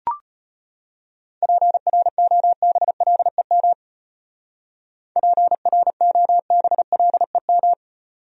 Morse code, heard as a keyed tone, about 31 wpm, 700 Hz. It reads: PROBLEM PROBLEM
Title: Speed-Racing Top 500 Words 25wpm